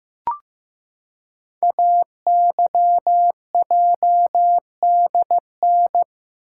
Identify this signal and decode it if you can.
Morse code, heard as a keyed tone, about 15 wpm, 700 Hz. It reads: AYJDN
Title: Code Groups - 5 Letters - 15wpm